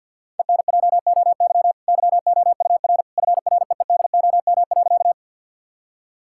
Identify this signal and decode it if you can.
Morse code, heard as a keyed tone, about 50 wpm, 700 Hz. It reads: ENJOY YOUR FREEDOM.
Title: Sentences from Top 1000 Words - Review Part 5 - 50wpm